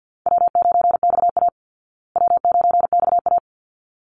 Morse code, heard as a keyed tone, about 50 wpm, 700 Hz. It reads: W9XA W9XA
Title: Call Signs Repeated 3x - 50wpm